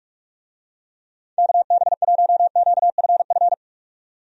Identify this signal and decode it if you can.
Morse code, heard as a keyed tone, about 47 wpm, 700 Hz. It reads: KB1QFF
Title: USA Call Signs - 6 Characters - 45wpm